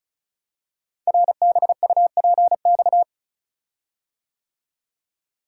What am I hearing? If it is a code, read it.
Morse code, heard as a keyed tone, about 36 wpm, 700 Hz. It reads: RBUPX
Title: Code Groups - 5 Letters - 35wpm